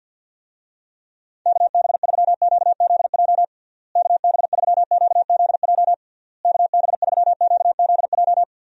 Morse code, heard as a keyed tone, about 52 wpm, 700 Hz. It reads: KB3QZJ KB3QZJ KB3QZJ
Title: USA Call Signs - 6 Characters - Repeated 3x - 50wpm